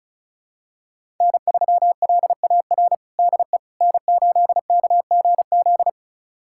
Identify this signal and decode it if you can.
Morse code, heard as a keyed tone, about 36 wpm, 700 Hz. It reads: N3LAR DE N8KGZ
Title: USA Callsign DE Callsign - 5 Characters - 35wpm